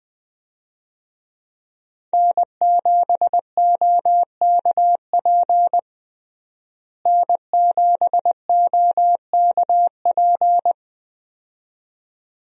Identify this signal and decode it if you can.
Morse code, heard as a keyed tone, about 21 wpm, 700 Hz. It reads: N7OKP N7OKP
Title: USA Call Signs - 5 Characters - Repeated 2x - 20wpm